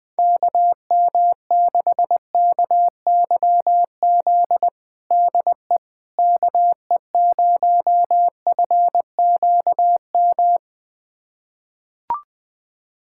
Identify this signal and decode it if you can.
Morse code, heard as a keyed tone, about 20 wpm, 700 Hz. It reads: KM6KYZ DE KE0FQM
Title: USA Callsign DE Callsign 20wpm